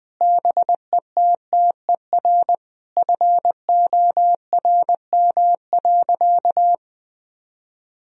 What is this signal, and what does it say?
Morse code, heard as a keyed tone, about 20 wpm, 700 Hz. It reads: BETTER FORM.
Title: Sentences from Top 1000 Words - Review Part 1 - 20wpm